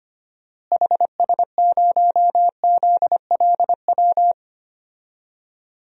Morse code, heard as a keyed tone, about 25 wpm, 700 Hz. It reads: HS0ZLW
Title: DXCC-100 Call Signs - 6 Characters - Repeated 2x - 25wpm